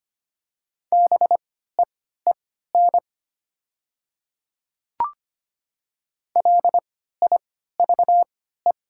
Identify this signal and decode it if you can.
Morse code, heard as a keyed tone, about 25 wpm, spaced as for 14 wpm, 700 Hz. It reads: BEEN LIVE
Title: 3x Character Spacing - Top 500 Words - 25wpm